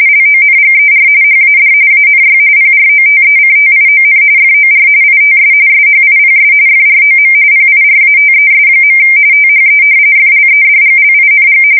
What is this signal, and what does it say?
RTTY, about 45.45 baud. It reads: LONGER HAD ANY PROTECTION FOR ITS SHIPS FROM PIRATES. THE FLEDGLING
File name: fsk170 / usnavy1